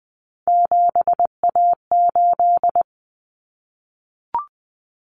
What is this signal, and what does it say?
Morse code, heard as a keyed tone, about 20 wpm, 700 Hz. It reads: 7A8
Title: Common Cut Numbers - 3-Digit 20wpm